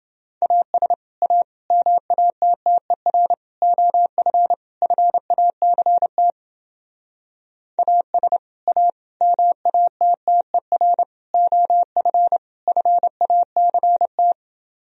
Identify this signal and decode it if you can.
Morse code, heard as a keyed tone, about 29 wpm, 700 Hz. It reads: AS A MATTER OF FACT AS A MATTER OF FACT